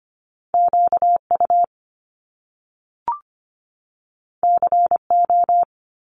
Morse code, heard as a keyed tone, about 25 wpm, 700 Hz. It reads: QU CO
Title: Top Two Letter Combinations in Words 25wpm